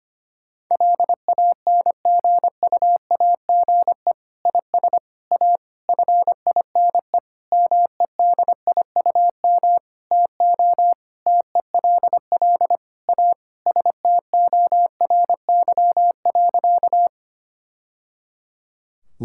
Morse code, heard as a keyed tone, about 25 wpm, 700 Hz. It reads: LANGUAGE IS A FINE MEDIUM TO TELL A STORY.
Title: Sentences from Top 1000 Words - Review Part 6 - 25wpm